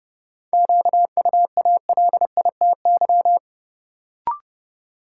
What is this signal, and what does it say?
Morse code, heard as a keyed tone, about 31 wpm, 700 Hz. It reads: QUALITY
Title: Sentences from Top 1000 Words - Review Part 4 - New Word Drill - 30wpm